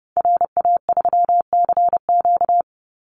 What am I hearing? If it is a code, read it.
Morse code, heard as a keyed tone, about 30 wpm, 700 Hz. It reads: RA3CQ
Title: Call Signs - 30wpm